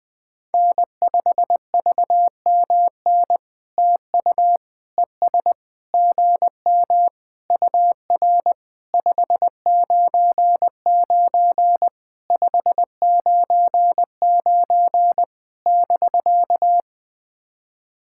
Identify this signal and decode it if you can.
Morse code, heard as a keyed tone, about 21 wpm, 700 Hz. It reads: N5VMN TU ES GM UR 599 599 <BK>
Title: POTA QSO Practice - 20wpm